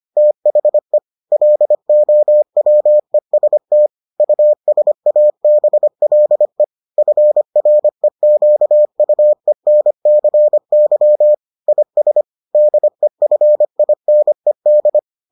Morse code, heard as a keyed tone, about 25 wpm, 600 Hz. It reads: THE LOWEST USABLE FREQUENCY IS DEFINED